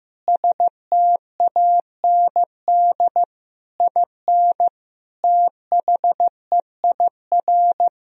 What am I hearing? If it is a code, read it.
Morse code, heard as a keyed tone, about 15 wpm, 700 Hz. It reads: STAND IN THEIR
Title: Sentences from Top 1000 Words - Review Part 1 - 15wpm